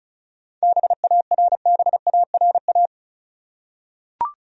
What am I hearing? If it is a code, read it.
Morse code, heard as a keyed tone, about 36 wpm, 700 Hz. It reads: BARBARA
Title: USA Common Names - 35wpm